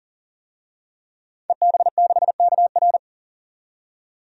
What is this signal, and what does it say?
Morse code, heard as a keyed tone, about 40 wpm, 700 Hz. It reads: EB6KR
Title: DXCC-100 Call Signs - 5 Characters - 40wpm